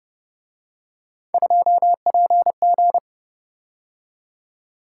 Morse code, heard as a keyed tone, about 31 wpm, 700 Hz. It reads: 2PG
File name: Speed-Racing - Three Letter-Number Combinations - 25wpm